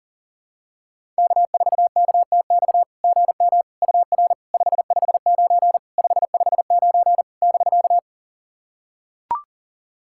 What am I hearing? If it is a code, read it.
Morse code, heard as a keyed tone, about 40 wpm, 700 Hz. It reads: K4KTX GM UR 559 559 <BK>
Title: SOTA QSO Practice 40wpm